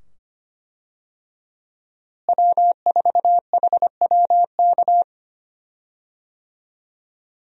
Morse code, heard as a keyed tone, about 25 wpm, 700 Hz. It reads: W4HWK
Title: USA Call Signs - 5 Characters - Repeated 2x - 25wpm